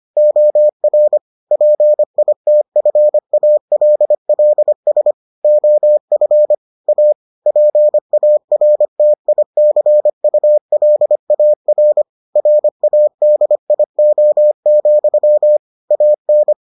Morse code, heard as a keyed tone, about 25 wpm, 600 Hz. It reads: OR PITFALLS OF A PARTICULAR RADIO, AN